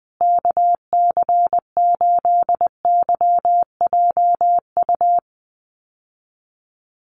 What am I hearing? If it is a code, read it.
Morse code, heard as a keyed tone, about 20 wpm, 700 Hz. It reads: KC8YJU
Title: USA Call Signs Repeated 3x 20wpm